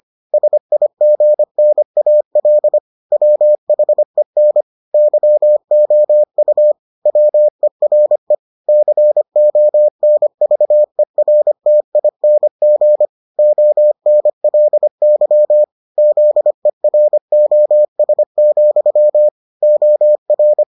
Morse code, heard as a keyed tone, about 25 wpm, 600 Hz. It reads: SIGNAL WHEN YOU WERE CONVERTING ONLY ZEROS, OR